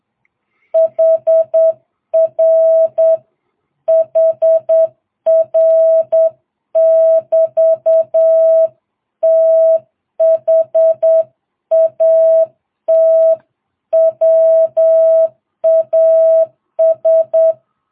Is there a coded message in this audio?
Yes: Morse code from 0.8 to 17.5 s, heard as a keyed tone, about 8 wpm, 660 Hz. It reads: HR HR <BT> THAT WAS